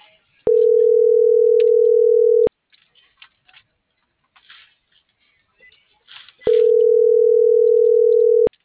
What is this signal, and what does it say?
Ringback tone (North America), 440 + 480 Hz.